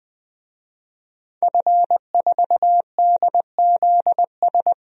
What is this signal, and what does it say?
Morse code, heard as a keyed tone, about 20 wpm, 700 Hz. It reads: F4DZS
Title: DXCC-100 Call Signs - 5 Characters - Repeated 3x - 20wpm